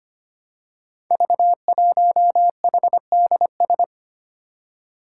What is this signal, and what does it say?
Morse code, heard as a keyed tone, about 25 wpm, 700 Hz. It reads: V1HDS
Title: Five Challenging Letter-Number Combinations 25wpm